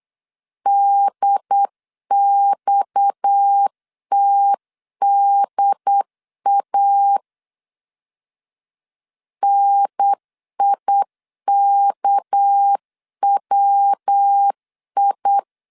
Morse code, heard as a keyed tone, about 8 wpm, 800 Hz. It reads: DXTDA NIKWI
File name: CW / DET / 1 6 1500C S C 8wpm 5219